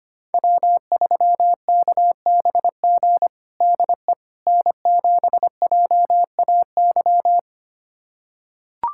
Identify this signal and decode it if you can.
Morse code, heard as a keyed tone, about 25 wpm, 700 Hz. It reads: W3KBG DE N7JAY
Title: USA Callsign DE Callsign - 5 Characters - 25wpm